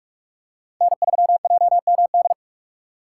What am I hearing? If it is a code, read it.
Morse code, heard as a keyed tone, about 46 wpm, 700 Hz. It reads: N3JMD
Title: USA Call Signs - 5 Characters - 2x Character Spacing - 45wpm